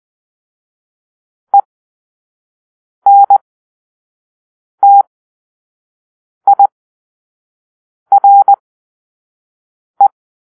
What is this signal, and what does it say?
Morse code, heard as a keyed tone, about 20 wpm, spaced as for 5 wpm, 800 Hz. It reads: ENTIRE
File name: arrl / Grp-2 Drill-2